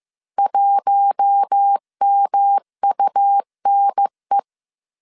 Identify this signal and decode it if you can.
Morse code, heard as a keyed tone, about 15 wpm, 800 Hz. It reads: 1MUNE
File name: CW / DET / 1 6 1500C S C 14wpm 2951